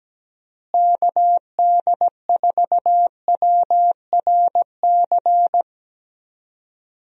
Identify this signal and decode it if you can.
Morse code, heard as a keyed tone, about 17 wpm, 700 Hz. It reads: KD4WRC
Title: USA Call Signs - 6 Characters - Repeated 3x - 17wpm